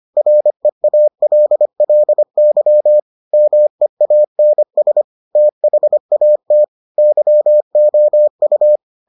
Morse code, heard as a keyed tone, about 25 wpm, 600 Hz. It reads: REALLY MEANS THAT YOU